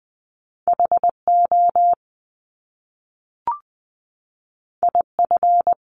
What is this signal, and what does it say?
Morse code, heard as a keyed tone, about 19 wpm, 700 Hz. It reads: H O I F
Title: Top Two Letter Combinations in Words 20wpm